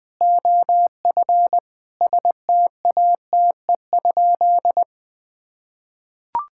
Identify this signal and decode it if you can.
Morse code, heard as a keyed tone, about 21 wpm, 700 Hz. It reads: OF STATE?
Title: Sentences from Top 1000 Words - Review Part 5 - 20wpm